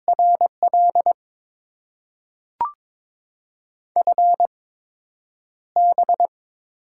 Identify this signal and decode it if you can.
Morse code, heard as a keyed tone, about 22 wpm, 700 Hz. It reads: R L F B
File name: Top 20 QSO Elements - 8x Character Spacing - 22wpm